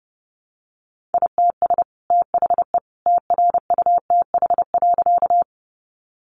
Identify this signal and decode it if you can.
Morse code, heard as a keyed tone, about 30 wpm, 700 Hz. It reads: ITS THE TRUTH.
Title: Sentences from Top 800 Words 30wpm